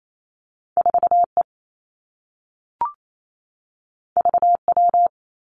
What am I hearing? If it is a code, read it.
Morse code, heard as a keyed tone, about 28 wpm, 700 Hz. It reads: 4 E V W
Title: Two Challenging Letter-Number Combinations 28wpm